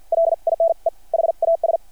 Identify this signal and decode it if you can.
Morse code, heard as a keyed tone, about 53 wpm, spaced as for 44 wpm, 660 Hz. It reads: LINE HAS
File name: archived / CRSnet